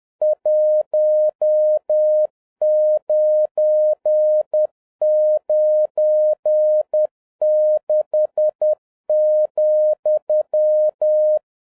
Morse code, heard as a keyed tone, about 10 wpm, 610 Hz. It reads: 1996,
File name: 2011-07-mp3 / n16 JAPAN TO CANCEL MORSE CODE EXAMINATION 10wpm